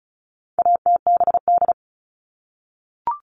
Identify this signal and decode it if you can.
Morse code, heard as a keyed tone, about 34 wpm, 700 Hz. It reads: ATBD
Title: All Cut Numbers - 4-Digit 35wpm